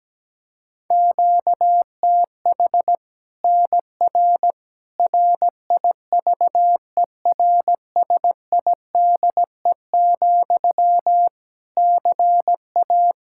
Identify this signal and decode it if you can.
Morse code, heard as a keyed tone, about 17 wpm, 700 Hz. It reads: QTH NR RIVERSIDE, CA
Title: Practice QSO - Exchange 1 - Beginning - 17wpm